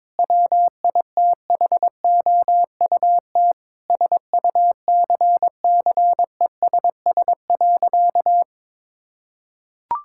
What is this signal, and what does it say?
Morse code, heard as a keyed tone, about 22 wpm, 700 Hz. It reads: WITHOUT SUCCESS.
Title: Sentences from Top 1000 Words - Review Part 6 - 22wpm